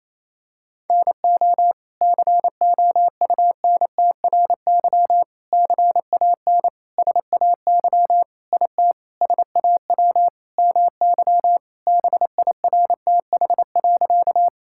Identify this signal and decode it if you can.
Morse code, heard as a keyed tone, about 28 wpm, 700 Hz. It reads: NO COUNTRY CAN SAY IT SAW MY BIRTH.
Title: Sentences from Top 1000 Words - Review Part 5 - 28wpm